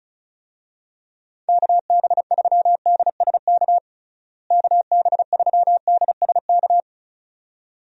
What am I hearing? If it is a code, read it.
Morse code, heard as a keyed tone, about 34 wpm, 700 Hz. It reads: KB3DSK KB3DSK
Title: USA Call Signs - 6 Characters - Repeated 2x - 35wpm